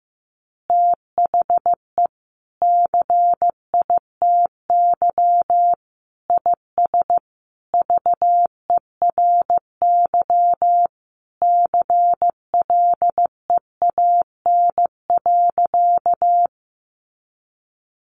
Morse code, heard as a keyed tone, about 15 wpm, 700 Hz. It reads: THE CITY IS VERY CLEAN.
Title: Easy Phrases 15wpm